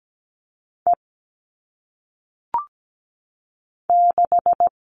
Morse code, heard as a keyed tone, about 17 wpm, 700 Hz. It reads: E 6
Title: All Cut Numbers - 1-Digit 17wpm